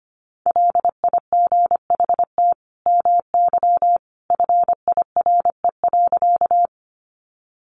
Morse code, heard as a keyed tone, about 25 wpm, 700 Hz. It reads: LIGHT MY FIRE.
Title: Sentences from Top 400 Words 25wpm